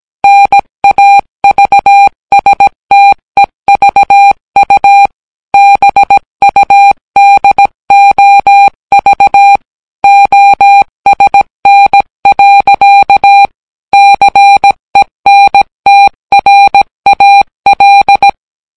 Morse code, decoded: NAVSTEVU BUDOV OSN. CENTRAL